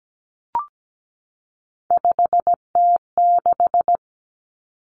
Morse code, heard as a keyed tone, about 17 wpm, 700 Hz. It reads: 5T6
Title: Common Cut Numbers - 3-Digit 17wpm